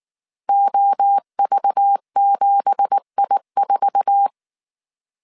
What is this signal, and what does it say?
Morse code, heard as a keyed tone, about 19 wpm, 800 Hz. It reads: OV7I4
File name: CW / DET / 1 9 1500C S A 18wpm 2459